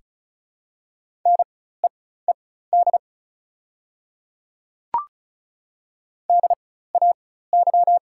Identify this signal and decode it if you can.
Morse code, heard as a keyed tone, about 36 wpm, spaced as for 16 wpm, 700 Hz. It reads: NEED DAY
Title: 4x Character Spacing - Top 200 Words - 35wpm